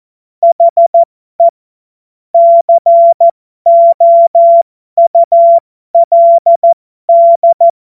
Morse code, heard as a keyed tone, about 14 wpm, spaced as for 12 wpm, 680 Hz. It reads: HE COULD
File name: Morsekurs / Witze